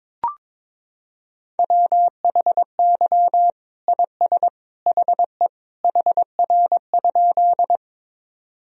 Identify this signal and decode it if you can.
Morse code, heard as a keyed tone, about 21 wpm, 700 Hz. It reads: WHY IS HE HR?
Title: Sentences from Top 200 Words and Abbreviations 22wpm